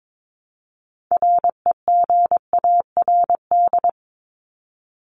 Morse code, heard as a keyed tone, about 22 wpm, 700 Hz. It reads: REGARD
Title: ED Suffix - Top 100 Words with Base Words 22wpm